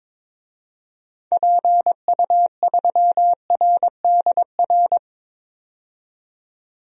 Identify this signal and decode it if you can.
Morse code, heard as a keyed tone, about 22 wpm, 700 Hz. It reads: PU3RDR
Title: DXCC-100 Call Signs - 6 Characters - 22wpm